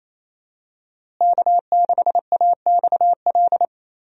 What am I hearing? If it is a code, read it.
Morse code, heard as a keyed tone, about 28 wpm, 700 Hz. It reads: K6AXL
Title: USA Call Signs - 5 Characters - Repeated 3x - 28wpm